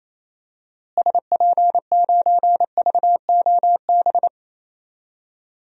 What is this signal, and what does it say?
Morse code, heard as a keyed tone, about 28 wpm, 700 Hz. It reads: SP9VOB
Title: DXCC-100 Call Signs - 6 Characters - Repeated 2x - 28wpm